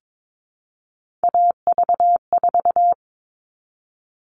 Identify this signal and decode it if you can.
Morse code, heard as a keyed tone, about 22 wpm, 700 Hz. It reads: AV4
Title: All Cut Numbers - 3-Digit 22wpm